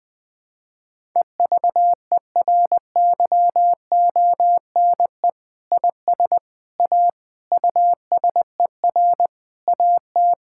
Morse code, heard as a keyed tone, about 21 wpm, 700 Hz. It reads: EVERYONE IS A USER AT
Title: Sentences from Top 1000 Words - Review Part 5 - 20wpm